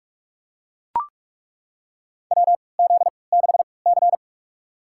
Morse code, heard as a keyed tone, about 42 wpm, spaced as for 25 wpm, 700 Hz. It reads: WZ6C
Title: USA Call Signs - 4 Characters - 3x Character Spacing - 45wpm